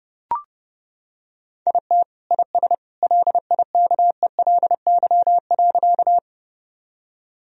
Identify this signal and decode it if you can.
Morse code, heard as a keyed tone, about 30 wpm, 700 Hz. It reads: IT IS LIKELY.
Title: Sentences from Top 1000 Words - Review Part 4 - 30wpm